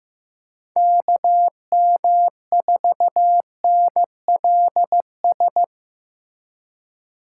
Morse code, heard as a keyed tone, about 15 wpm, 700 Hz. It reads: KM4NLS
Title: USA Call Signs - 6 Characters - 6x Character Spacing - 15wpm